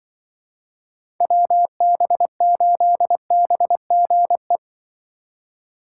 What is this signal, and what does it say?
Morse code, heard as a keyed tone, about 24 wpm, 700 Hz. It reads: WB8BGE
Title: Speed Racing - USA 2x3 Call Signs 20wpm